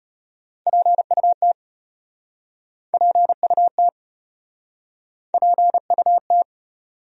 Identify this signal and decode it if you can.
Morse code, heard as a keyed tone, about 37 wpm, 700 Hz. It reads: PUT PUT PUT